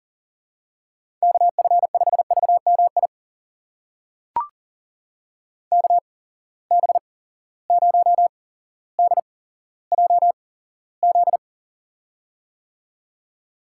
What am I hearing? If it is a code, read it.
Morse code, heard as a keyed tone, about 40 wpm, spaced as for 17 wpm, 700 Hz. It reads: KF5VMI KB0DJZ
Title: USA Call Signs - 6 Characters - 8x Character Spacing - 40wpm